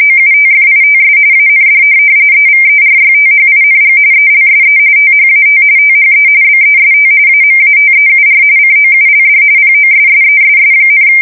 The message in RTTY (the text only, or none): OUTFITS AND EVERYTHING ELSE THAT THE BEST AMATEUR COULD POSSESS,